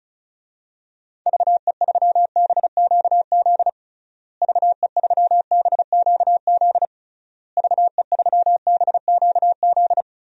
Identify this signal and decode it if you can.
Morse code, heard as a keyed tone, about 36 wpm, 700 Hz. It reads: VE3BQZ VE3BQZ VE3BQZ
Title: DXCC-100 Call Signs - 6 Characters - Repeated 3x - 35wpm